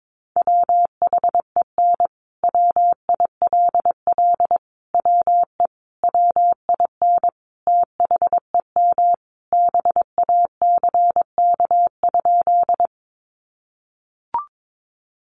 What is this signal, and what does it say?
Morse code, heard as a keyed tone, about 22 wpm, 700 Hz. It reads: WHEN WILL WE WIN THEM BACK?
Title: Sentences from Top 1500 Words 22wpm